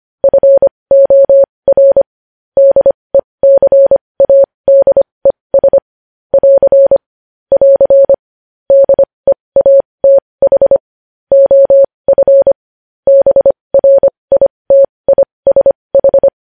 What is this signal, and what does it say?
Morse code, heard as a keyed tone, about 25 wpm, spaced as for 20 wpm, 550 Hz. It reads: FOR DECADES <AR> <AR> DEATH OF BRITISH
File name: podcast / News headlines-1x-25x20